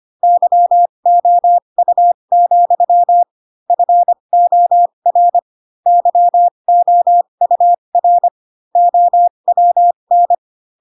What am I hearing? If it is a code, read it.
Morse code, heard as a keyed tone, about 25 wpm, spaced as for 22 wpm, 700 Hz. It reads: YOU, FOR YOUR OWN
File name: Pinocchio / Pinocchio ch4 18 25 ik1hge